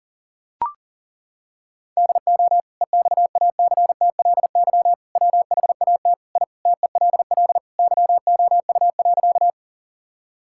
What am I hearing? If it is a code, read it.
Morse code, heard as a keyed tone, about 40 wpm, 700 Hz. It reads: DO EXACTLY WHAT I TELL YOU.
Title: Sentences from Top 1000 Words - Review Part 6 - 40wpm